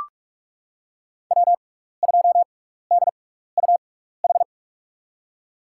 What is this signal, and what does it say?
Morse code, heard as a keyed tone, about 45 wpm, spaced as for 15 wpm, 700 Hz. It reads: W2DUH
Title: USA Call Signs - 5 Characters - 6x Character Spacing - 45wpm